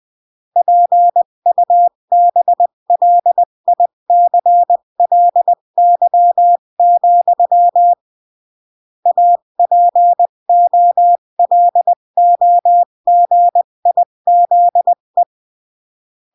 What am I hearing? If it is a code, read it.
Morse code, heard as a keyed tone, about 20 wpm, 700 Hz. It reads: PUBLICLY, APOLOGIZE
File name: CW / jodorowsky / 82 commandments 20-15